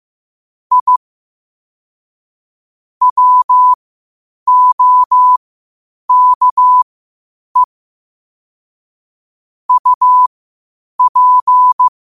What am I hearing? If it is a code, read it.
Morse code, heard as a keyed tone, about 15 wpm, spaced as for 8 wpm, 1000 Hz. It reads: I WOKE UP
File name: qotdfiles / 8wpm